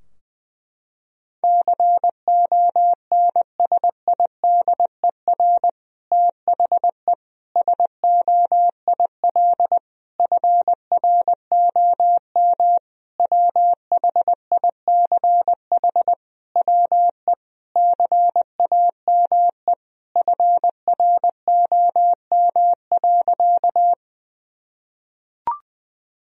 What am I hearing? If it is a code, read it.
Morse code, heard as a keyed tone, about 19 wpm, 700 Hz. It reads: CONSIDER THE SOIL FROM WHICH WE CAME FROM.
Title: Sentences from Top 1000 Words - Review Part 4 - 20wpm